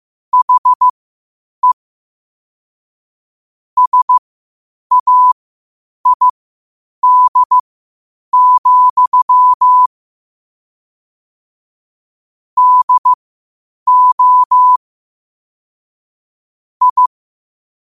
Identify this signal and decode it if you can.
Morse code, heard as a keyed tone, about 15 wpm, spaced as for 8 wpm, 1000 Hz. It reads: HE SAID, DO I
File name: qotdfiles / 8wpm